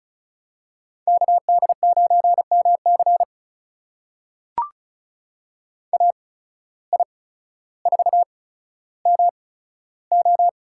Morse code, heard as a keyed tone, about 36 wpm, spaced as for 15 wpm, 700 Hz. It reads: KD9MC AI4MO
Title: USA Call Signs - 5 Characters - 8x Character Spacing - 35wpm